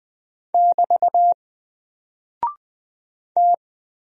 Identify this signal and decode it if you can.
Morse code, heard as a keyed tone, about 20 wpm, 700 Hz. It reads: <BT> T
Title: Single Character (Letter:Number:Punctuation:Prosign) Alt 20wpm